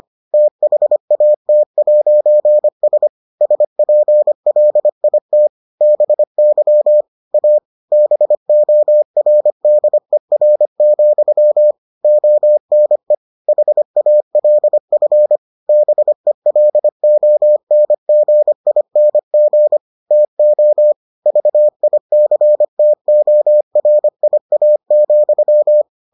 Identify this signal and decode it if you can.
Morse code, heard as a keyed tone, about 25 wpm, 600 Hz. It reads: THAT'S SPLIT BY A BORDER, ONE HALF BELONGING TO VICTORIA,